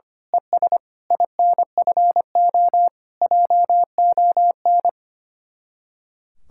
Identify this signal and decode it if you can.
Morse code, heard as a keyed tone, about 24 wpm, 700 Hz. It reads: ES INFO JON
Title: Speed-Racing Common QSO Phrases - 25wpm